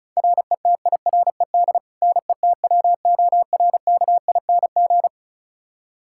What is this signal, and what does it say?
Morse code, heard as a keyed tone, about 36 wpm, 700 Hz. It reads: RETIRED NETWORKING
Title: K1USN SST QSO Practice 35wpm